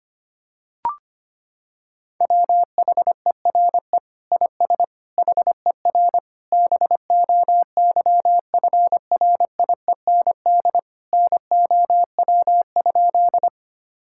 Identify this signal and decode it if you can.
Morse code, heard as a keyed tone, about 25 wpm, 700 Hz. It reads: WHERE IS HER BOYFRIEND NOW?
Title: Sentences from Top 1000 Words - Review Part 5 - 25wpm